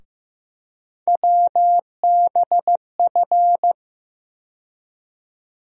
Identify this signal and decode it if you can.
Morse code, heard as a keyed tone, about 15 wpm, 700 Hz. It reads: WBF
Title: USA Call Sign Postfixes - 3 Letters 15wpm